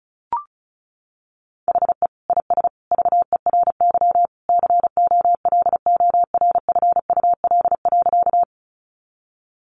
Morse code, heard as a keyed tone, about 34 wpm, 700 Hz. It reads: HE IS VERY COLORFUL.
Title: Easy Phrases 35wpm